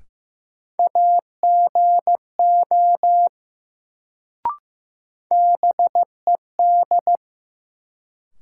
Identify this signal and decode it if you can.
Morse code, heard as a keyed tone, about 15 wpm, 700 Hz. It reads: AGO BED
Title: Sentences from Top 500 Words - New Word Drill - 15wpm